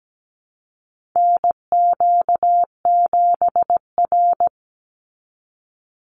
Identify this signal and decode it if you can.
Morse code, heard as a keyed tone, about 17 wpm, 700 Hz. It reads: NQ7R
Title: NA SOTA Chaser Callsigns 17wpm